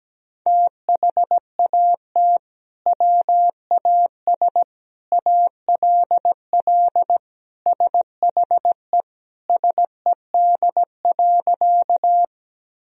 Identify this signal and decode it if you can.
Morse code, heard as a keyed tone, about 17 wpm, 700 Hz. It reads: THAT WAS ALL SHE SED.
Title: Sentences from Top 100 Words and Abbreviations 17wpm